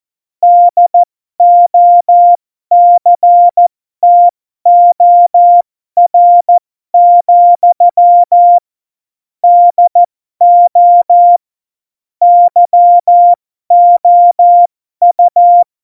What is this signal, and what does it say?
Morse code, heard as a keyed tone, about 14 wpm, spaced as for 12 wpm, 700 Hz. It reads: DOCTOR, DO YOU
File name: Morsekurs / Witze